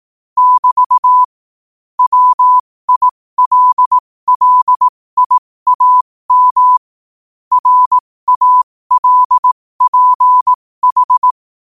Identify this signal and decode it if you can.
Morse code, heard as a keyed tone, about 18 wpm, spaced as for 15 wpm, 1000 Hz. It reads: <BT> WILLIAM RALPH
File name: qotdfiles / 15wpm